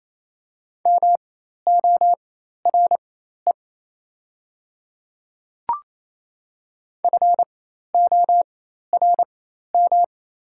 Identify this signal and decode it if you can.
Morse code, heard as a keyed tone, about 28 wpm, spaced as for 13 wpm, 690 Hz. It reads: MORE FORM
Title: 4x Character Spacing - Top 200 Words - 28wpm